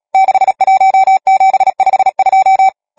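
Morse code, heard as a keyed tone, about 37 wpm, 770 Hz. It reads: 61752
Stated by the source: Morse code number practice, machine-sent training text (Nácvik telegrafie, OK1FCB).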